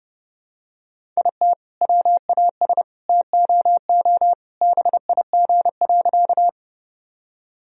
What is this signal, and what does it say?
Morse code, heard as a keyed tone, about 30 wpm, 700 Hz. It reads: IT WAS TOO BIG.
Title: Sentences from Top 1000 Words - Review Part 1 - 30wpm